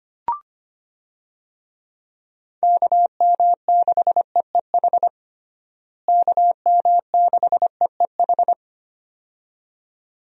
Morse code, heard as a keyed tone, about 25 wpm, 700 Hz. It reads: KM6EEH KM6EEH
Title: USA Call Signs - 6 Characters - Repeated 2x - 25wpm